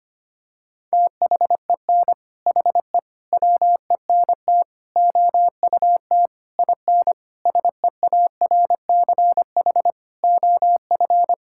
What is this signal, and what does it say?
Morse code, heard as a keyed tone, about 25 wpm, 700 Hz. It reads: THEN HE WENT OUT IN SEARCH OF
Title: Sentences from Top 1000 Words - Review Part 5 - 25wpm